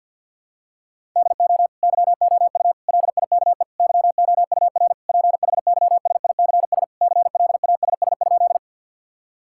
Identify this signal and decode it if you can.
Morse code, heard as a keyed tone, about 52 wpm, 700 Hz. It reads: DO YOU LIKE YOUR PHYSICS CLASS?
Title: Sentences from Top 1000 Words - Review Part 5 - 50wpm